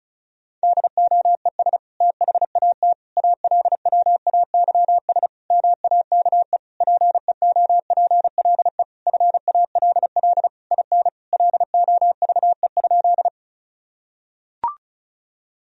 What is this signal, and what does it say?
Morse code, heard as a keyed tone, about 34 wpm, 700 Hz. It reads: DOES THAT ALWAYS MAKE PEOPLE FALL IN LOVE?
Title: Sentences from Top 1000 Words - Review Part 2 - 35wpm